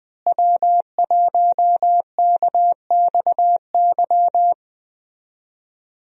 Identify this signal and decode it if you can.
Morse code, heard as a keyed tone, about 20 wpm, 700 Hz. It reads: W1KXY
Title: USA Call Signs - 5 Characters - Repeated 3x - 20wpm